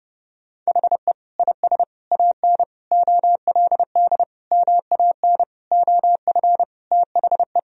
Morse code, heard as a keyed tone, about 30 wpm, 700 Hz. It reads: HE IS AN OLD MAN OF THE
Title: Sentences from Top 1000 Words - Review Part 2 - 30wpm